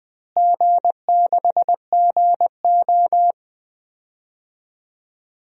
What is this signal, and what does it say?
Morse code, heard as a keyed tone, about 21 wpm, 700 Hz. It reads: G6GO
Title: DXCC-100 Call Signs - 4 Characters - Repeated 2x - 20wpm